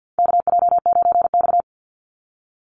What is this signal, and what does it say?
Morse code, heard as a keyed tone, about 48 wpm, 700 Hz. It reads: KJ9X
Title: NA SOTA Chaser Callsigns 50wpm